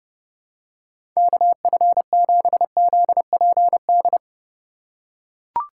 Morse code, heard as a keyed tone, about 32 wpm, 700 Hz. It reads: KF7ZPD
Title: USA Call Signs - 3x Character Spacing - 30wpm